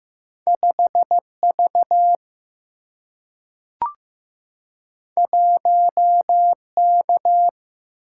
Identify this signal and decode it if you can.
Morse code, heard as a keyed tone, about 15 wpm, 700 Hz. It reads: 5 V 1 K
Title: Single-Letter Single-Number 15wpm